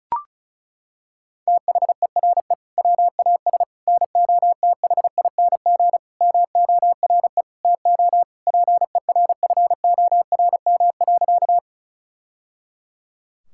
Morse code, heard as a keyed tone, about 34 wpm, 700 Hz. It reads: THERE WAS NOTHING MORE TO PERFORM.
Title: Sentences from Top 1000 Words - Review Part 5 - 35wpm